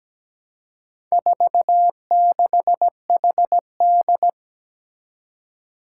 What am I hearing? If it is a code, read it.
Morse code, heard as a keyed tone, about 17 wpm, 700 Hz. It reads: 46HD
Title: Four Challenging Letter-Number Combinations 17wpm